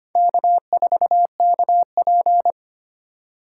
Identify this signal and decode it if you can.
Morse code, heard as a keyed tone, about 25 wpm, 700 Hz. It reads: K4KP
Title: USA Call Signs - 4 Characters - 3x Character Spacing - 25wpm